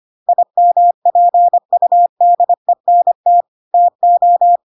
Morse code, heard as a keyed tone, about 25 wpm, 700 Hz. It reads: IMPUDENT TO
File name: Pinocchio / Pinocchio ch3 25 25 ik1hge